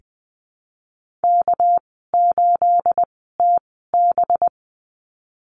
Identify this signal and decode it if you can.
Morse code, heard as a keyed tone, about 20 wpm, spaced as for 15 wpm, 700 Hz. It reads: K8TB
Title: 238 Lesson 38g - Letter q - Call Signs 20wpm